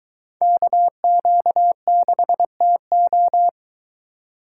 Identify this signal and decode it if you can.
Morse code, heard as a keyed tone, about 23 wpm, 700 Hz. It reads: KQ6TO